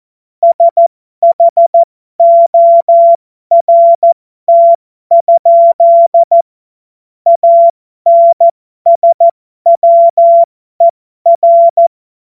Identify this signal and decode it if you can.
Morse code, heard as a keyed tone, about 14 wpm, spaced as for 12 wpm, 680 Hz. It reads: SHORT? ANSWER
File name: Morsekurs / Witze